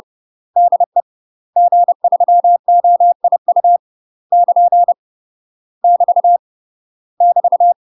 Morse code, heard as a keyed tone, about 30 wpm, 700 Hz. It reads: DE G3OIU <KN> <BT> <BT>
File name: UK MASTER QSOs TRAINING / 30WPM / UK MASTER QSOs 0029